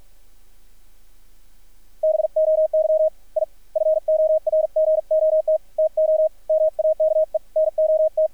Morse code, heard as a keyed tone, about 50 wpm, 630 Hz, with radio noise. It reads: BOY I FORGOT TO MAKE NOT
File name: archived / CRSnet